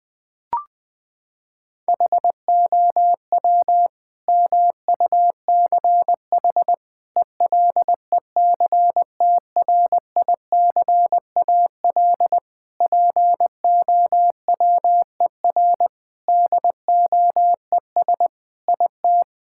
Morse code, heard as a keyed tone, about 20 wpm, 700 Hz. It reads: HOW MUCH ELECTRICAL POWER DOES IT
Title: Sentences from Top 1000 Words - Review Part 6 - 20wpm